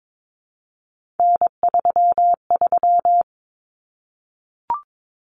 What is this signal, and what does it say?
Morse code, heard as a keyed tone, about 22 wpm, 700 Hz. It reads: N33
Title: Common Cut Numbers - 3-Digit 22wpm